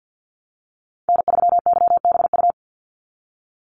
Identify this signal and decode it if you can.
Morse code, heard as a keyed tone, about 50 wpm, 700 Hz. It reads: N3YBU
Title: USA Call Signs 50wpm